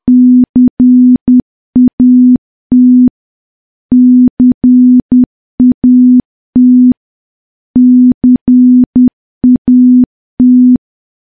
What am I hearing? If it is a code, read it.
Morse code, heard as a keyed tone, about 10 wpm, 250 Hz. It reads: CAT CAT CAT
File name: morse / morse-training-10WPM